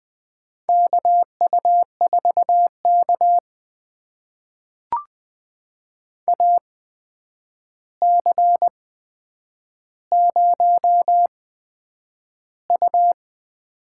Morse code, heard as a keyed tone, about 20 wpm, spaced as for 9 wpm, 700 Hz. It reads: KU4K AC0U
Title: USA Call Signs - 4 Characters - 8x Character Spacing - 20wpm